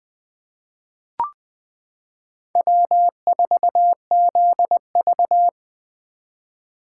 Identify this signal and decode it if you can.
Morse code, heard as a keyed tone, about 20 wpm, 700 Hz. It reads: W4ZV
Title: 2023 SOTA Chaser Call Signs - 20wpm